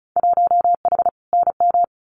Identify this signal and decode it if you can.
Morse code, heard as a keyed tone, about 34 wpm, 700 Hz. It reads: 1H NM
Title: Winter Field Day Exchanges 35wpm